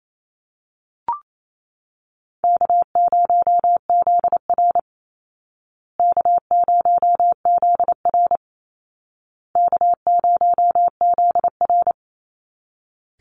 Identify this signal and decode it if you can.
Morse code, heard as a keyed tone, about 28 wpm, 700 Hz. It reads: K0ZR K0ZR K0ZR
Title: Easy Call Signs Repeated 3x - 28wpm